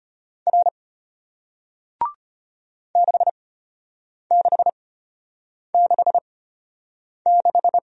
Morse code, heard as a keyed tone, about 33 wpm, spaced as for 8 wpm, 700 Hz. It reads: R 6666